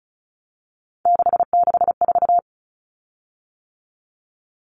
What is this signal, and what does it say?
Morse code, heard as a keyed tone, about 34 wpm, 700 Hz. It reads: 664
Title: Common Cut Numbers - 3-Digit 35wpm